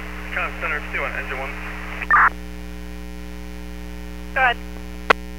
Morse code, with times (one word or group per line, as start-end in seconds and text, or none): none